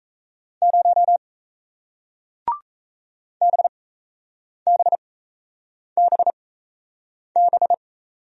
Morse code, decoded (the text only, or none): 0 BBBB